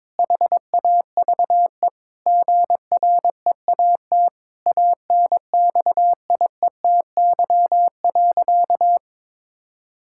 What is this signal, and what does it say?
Morse code, heard as a keyed tone, about 22 wpm, 700 Hz. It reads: HAVE GREAT ANXIETY.
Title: Sentences from Top 1600 Words 22wpm